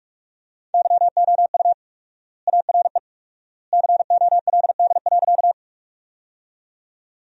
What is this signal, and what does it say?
Morse code, heard as a keyed tone, about 45 wpm, 700 Hz. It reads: YOU ARE COLD.
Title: 4x Word-Spacing - Sentences from Top 400 Words - 45wpm